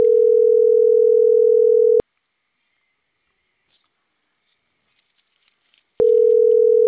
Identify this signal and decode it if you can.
Ringback tone (North America), 440 + 480 Hz.